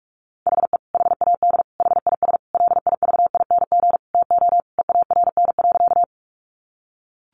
Morse code, heard as a keyed tone, about 44 wpm, 700 Hz. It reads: HE HAD HIS LIVING TO EARN.
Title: Sentences from Top 1500 Words 45wpm